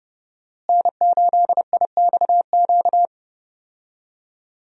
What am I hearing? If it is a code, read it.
Morse code, heard as a keyed tone, about 30 wpm, 700 Hz. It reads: N8IXQ
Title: USA Call Signs - 5 Characters - Repeated 2x - 30wpm